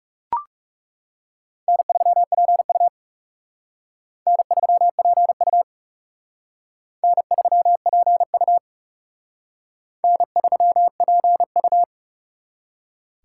Morse code, heard as a keyed tone, about 36 wpm, 700 Hz. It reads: N3PU N3PU N3PU N3PU
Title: Speed Racing - USA 1x2 Call Signs 30wpm